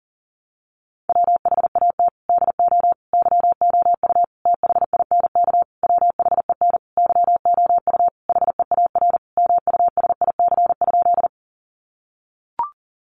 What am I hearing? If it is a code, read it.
Morse code, heard as a keyed tone, about 40 wpm, 700 Hz. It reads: WHAT DO YOU THINK WHEN YOU HEAR MUSIC?
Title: Sentences from Top 300 Words 40wpm